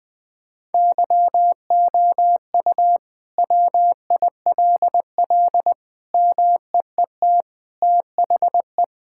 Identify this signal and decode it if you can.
Morse code, heard as a keyed tone, about 20 wpm, 700 Hz. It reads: YOU WILL MEET THE
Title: Sentences with 6-Word Phrases - Spoken Beginning - 20wpm